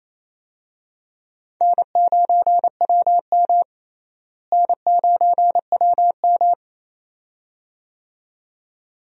Morse code, heard as a keyed tone, about 27 wpm, 700 Hz. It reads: N9WM N9WM
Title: USA Call Signs - 4 Characters - Repeated 2x - 28wpm